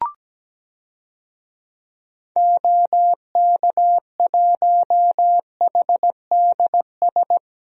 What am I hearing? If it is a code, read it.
Morse code, heard as a keyed tone, about 17 wpm, 700 Hz. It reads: OK1HDS
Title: DXCC-100 Call Signs - 6 Characters - Repeated 3x - 17wpm